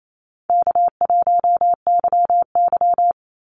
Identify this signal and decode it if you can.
Morse code, heard as a keyed tone, about 28 wpm, 700 Hz. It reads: K1YY
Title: Easy Call Signs Repeated 2x - 28wpm